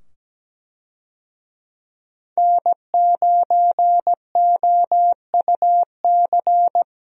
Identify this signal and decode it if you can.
Morse code, heard as a keyed tone, about 17 wpm, 700 Hz. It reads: N9OUC
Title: USA Call Signs - 5 Characters - 17wpm